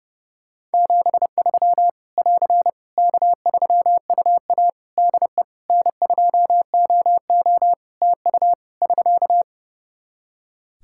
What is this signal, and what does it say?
Morse code, heard as a keyed tone, about 32 wpm, 700 Hz. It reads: 73 <AR> K3UA DE N2OO TU <SK>
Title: K1USN SST QSO Practice 30wpm